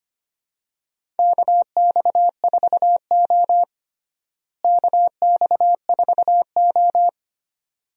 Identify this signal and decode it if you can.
Morse code, heard as a keyed tone, about 25 wpm, 700 Hz. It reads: KX4O KX4O
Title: USA Call Signs - 4 Characters - Repeated 2x - 25wpm